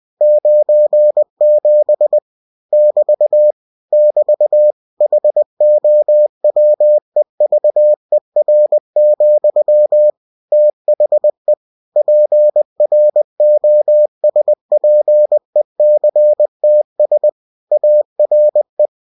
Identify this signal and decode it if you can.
Morse code, heard as a keyed tone, about 20 wpm, 600 Hz. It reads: 97 <BT> <BT> HOWEVER, THE PROSPECTS ARE
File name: cw-books / winnie-the-pooh / winnie-20wpm-page-0097